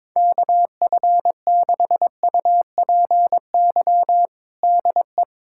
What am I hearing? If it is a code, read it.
Morse code, heard as a keyed tone, about 22 wpm, 700 Hz. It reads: KF6UPY DE
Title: USA Callsign DE Callsign - 6 Characters - 22wpm